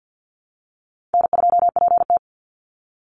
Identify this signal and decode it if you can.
Morse code, heard as a keyed tone, about 50 wpm, 700 Hz. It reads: N2PT
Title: USA Call Signs 50wpm